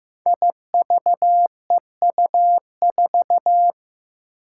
Morse code, heard as a keyed tone, about 15 wpm, 700 Hz. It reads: IVEU4
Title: Five Challenging Letter-Number Combinations 15wpm